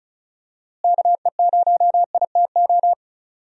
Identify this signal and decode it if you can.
Morse code, heard as a keyed tone, about 36 wpm, 700 Hz. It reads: KE0ITO
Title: USA Call Signs - 6 Characters - Repeated 2x - 35wpm